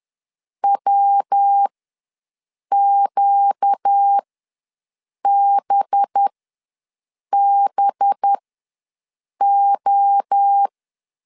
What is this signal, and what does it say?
Morse code, heard as a keyed tone, about 10 wpm, spaced as for 6 wpm, 800 Hz. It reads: WQBBO
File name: LEZIONI / 5 LBVC / 5 LB ws 10wpm 3616